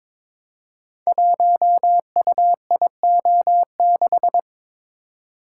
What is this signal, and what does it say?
Morse code, heard as a keyed tone, about 22 wpm, 700 Hz. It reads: 1UIO6
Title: Code Groups - 5 Letter:Numbers - 22wpm